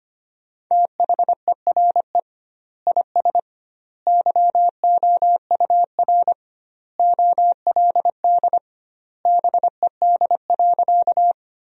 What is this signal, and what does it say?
Morse code, heard as a keyed tone, about 25 wpm, 700 Hz. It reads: THERE IS YOUR OLD BED.
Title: 2x Word-Spacing - Sentences from Top 500 Words - 25wpm